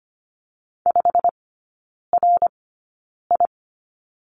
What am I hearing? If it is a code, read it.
Morse code, heard as a keyed tone, about 25 wpm, spaced as for 9 wpm, 690 Hz. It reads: 5RI
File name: Lesson 15e - Number 5 - Three Characters 25wpm